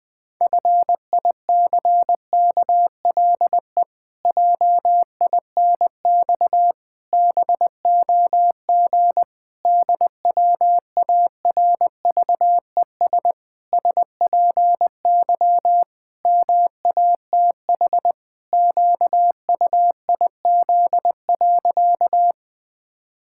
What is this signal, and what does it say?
Morse code, heard as a keyed tone, about 20 wpm, 700 Hz. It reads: FICKLE JINX BOG DWARVES SPY MATH QUIZ.
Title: Pangram Challenge v2 - 20wpm